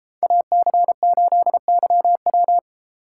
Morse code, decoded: AC8YW

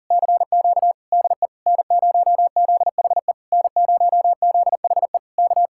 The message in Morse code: CQ DE N0ZHE N0ZHE K

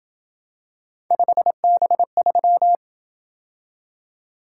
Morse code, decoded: 5B3

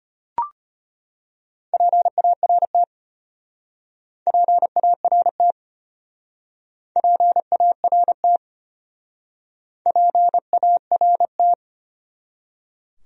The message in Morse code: PART PART PART PART

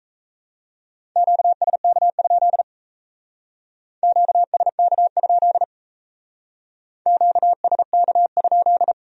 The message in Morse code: QSK? QSK? QSK?

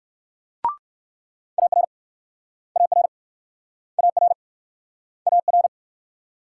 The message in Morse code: AR AR AR AR

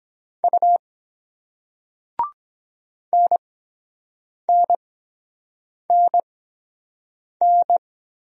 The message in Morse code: U NNNN

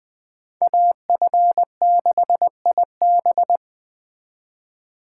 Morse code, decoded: AF6IB

